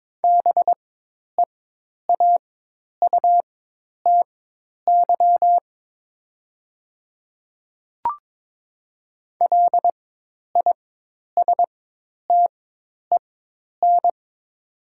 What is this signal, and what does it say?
Morse code, heard as a keyed tone, about 22 wpm, spaced as for 10 wpm, 700 Hz. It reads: BEAUTY LISTEN
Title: 4x Character Spacing - Top 500 Words - 22wpm